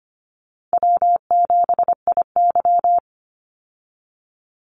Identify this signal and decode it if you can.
Morse code, heard as a keyed tone, about 25 wpm, 700 Hz. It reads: W7IY
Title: Easy Call Signs Repeated 2x - 25wpm